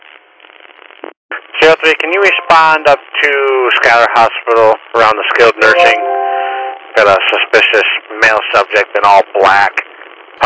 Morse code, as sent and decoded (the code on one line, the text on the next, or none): none